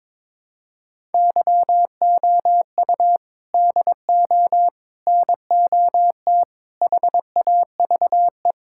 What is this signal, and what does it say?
Morse code, heard as a keyed tone, about 22 wpm, 700 Hz. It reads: YOU DO NOT HAVE
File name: practice / 22wpm / Build A Phrase - 5-Word Phrases - 22wpm